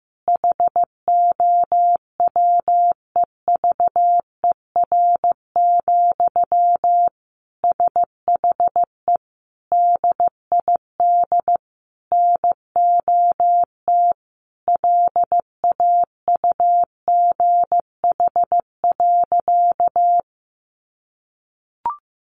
Morse code, heard as a keyed tone, about 14 wpm, 700 Hz. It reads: HOWEVER, SHE DID NOT LAUGH.
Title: Sentences from Top 1100 Words 15wpm